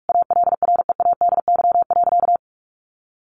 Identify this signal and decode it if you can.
Morse code, heard as a keyed tone, about 44 wpm, 700 Hz. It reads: ALREADY.
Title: Sentences from Top 700 Words 45wpm